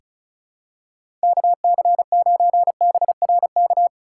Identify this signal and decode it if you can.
Morse code, heard as a keyed tone, about 34 wpm, 700 Hz. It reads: KC9BRK
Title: USA Call Signs - 6x Character Spacing - 35wpm